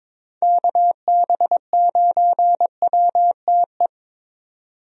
Morse code, heard as a keyed tone, about 22 wpm, 700 Hz. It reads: KB9WTE
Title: USA Call Signs - 6 Characters - 22wpm